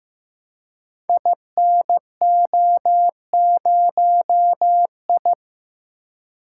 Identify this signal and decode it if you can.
Morse code, heard as a keyed tone, about 15 wpm, 700 Hz. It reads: INO0I
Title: Code Groups - 5 Letter:Numbers - 15wpm